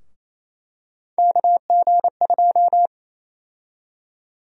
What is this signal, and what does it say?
Morse code, heard as a keyed tone, about 28 wpm, 700 Hz. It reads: KG2
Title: USA Call Sign Prefixes - Repeated 2x - 28wpm